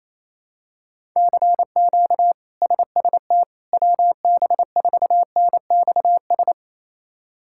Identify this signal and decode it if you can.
Morse code, heard as a keyed tone, about 29 wpm, 700 Hz. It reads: CQ SST WB4NXS
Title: K1USN SST QSO Practice - 28wpm